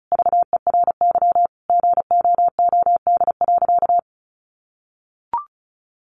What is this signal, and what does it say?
Morse code, heard as a keyed tone, about 34 wpm, 700 Hz. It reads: VERY GOOD.
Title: Sentences from Top 200 Words 35wpm